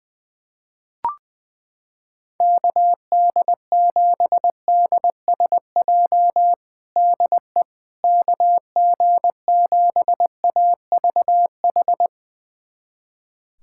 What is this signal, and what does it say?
Morse code, heard as a keyed tone, about 20 wpm, 700 Hz. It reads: KD7DSJ DE KG7AVH
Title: USA Callsign DE Callsign 20wpm